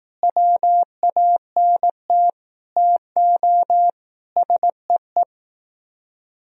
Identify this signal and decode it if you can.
Morse code, heard as a keyed tone, about 18 wpm, 700 Hz. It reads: WANT TO SEE
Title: Speed-Racing - Sets of 5 Words 15wpm